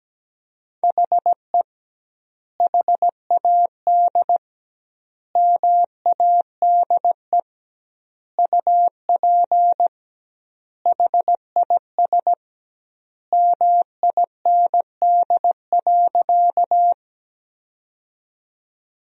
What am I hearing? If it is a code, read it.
Morse code, heard as a keyed tone, about 17 wpm, 700 Hz. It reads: HE HAD MADE UP HIS MIND.
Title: 2x Word-Spacing - Sentences from Top 500 Words - 17wpm